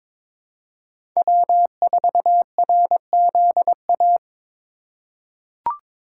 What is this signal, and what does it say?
Morse code, heard as a keyed tone, about 22 wpm, 700 Hz. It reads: W4RZA
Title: USA Call Signs - 5 Characters - 3x Character Spacing - 22wpm